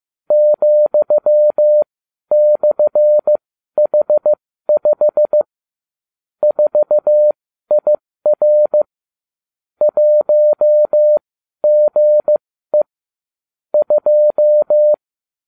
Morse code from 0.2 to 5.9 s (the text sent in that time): ,/H5